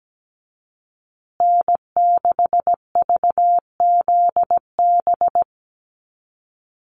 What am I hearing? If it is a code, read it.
Morse code, heard as a keyed tone, about 17 wpm, 700 Hz. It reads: N6VZB
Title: USA Call Signs Repeated 3x 17wpm